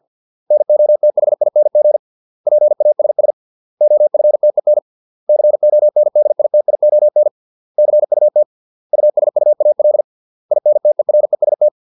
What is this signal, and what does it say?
Morse code, heard as a keyed tone, about 50 wpm, 600 Hz. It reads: NOTHING PASS OFTEN CONDITION CUT USUAL INTEREST
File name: 632 words / 632 Words 50WPM